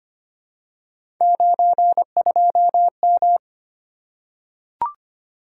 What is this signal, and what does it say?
Morse code, heard as a keyed tone, about 25 wpm, 700 Hz. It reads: 92M
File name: Three Letter-Number Combinations - 25wpm